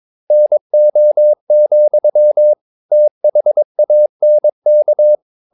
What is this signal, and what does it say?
Morse code, heard as a keyed tone, about 22 wpm, 600 Hz. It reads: NO, THANK
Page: Romanen Never af Ken Follet på engelsk.